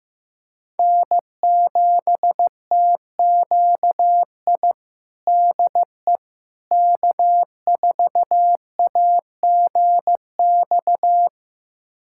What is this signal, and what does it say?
Morse code, heard as a keyed tone, about 16 wpm, 700 Hz. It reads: N7TQI DE K4AGX
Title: USA Callsign DE Callsign 15wpm